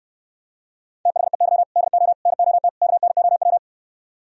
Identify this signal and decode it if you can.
Morse code, heard as a keyed tone, about 67 wpm, 700 Hz. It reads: THEY DO NOT KNOW